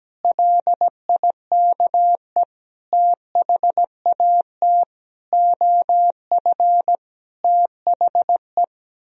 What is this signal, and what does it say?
Morse code, heard as a keyed tone, about 17 wpm, 700 Hz. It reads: LIKE THAT OF THE
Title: Sentences from Top 1000 Words - Review Part 2 - 17wpm